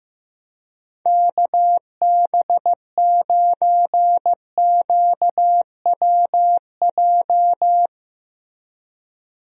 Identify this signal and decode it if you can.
Morse code, heard as a keyed tone, about 15 wpm, 700 Hz. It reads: KB9QWJ
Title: USA Call Signs - 6 Characters - Repeated 2x - 15wpm